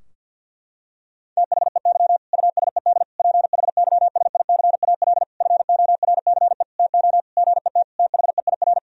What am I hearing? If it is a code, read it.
Morse code, heard as a keyed tone, about 50 wpm, 700 Hz. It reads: THEY USED PHYSICAL FORCE TO GET THEIR
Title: Sentences from Top 1000 Words - Review Part 4 - 50wpm